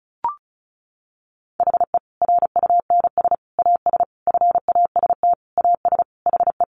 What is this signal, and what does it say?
Morse code, heard as a keyed tone, about 34 wpm, 700 Hz. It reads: HE RUNS AS FAST AS HE
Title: Additional Easy Phrases 35wpm